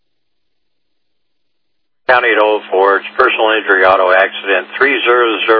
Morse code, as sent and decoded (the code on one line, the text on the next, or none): none